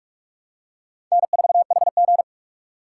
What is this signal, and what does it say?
Morse code, heard as a keyed tone, about 46 wpm, 700 Hz. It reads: N4HG
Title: USA Call Signs - 4 Characters - 4x Character Spacing - 45wpm